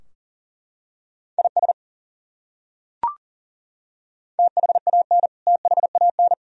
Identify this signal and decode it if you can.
Morse code, heard as a keyed tone, about 38 wpm, 700 Hz. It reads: IS THAN THAN
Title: Top 100 Words Repeated 40wpm